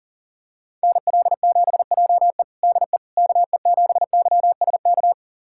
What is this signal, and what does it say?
Morse code, heard as a keyed tone, about 40 wpm, 700 Hz. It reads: NL7JE DE KE7YSK
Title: USA Callsign DE Callsign 40wpm